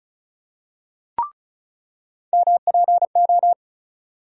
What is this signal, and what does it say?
Morse code, heard as a keyed tone, about 34 wpm, 700 Hz. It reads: MPO
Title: USA Call Sign Postfixes - 3 Letters 35wpm